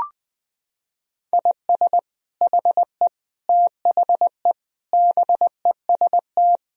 Morse code, heard as a keyed tone, about 21 wpm, 700 Hz. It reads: IS HE THE BEST
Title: Sentences from Top 1000 Words - Review Part 4 - 20wpm